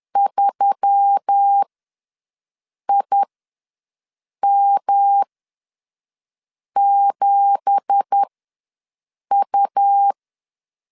Morse code, heard as a keyed tone, about 10 wpm, spaced as for 5 wpm, 800 Hz. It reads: 3IM7U
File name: LEZIONI / 9 2867 ws 10wpm 4215